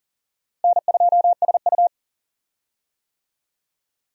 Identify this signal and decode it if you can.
Morse code, heard as a keyed tone, about 40 wpm, 700 Hz. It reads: N2SU
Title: USA Call Signs - 4 Characters - Repeated 3x - 40wpm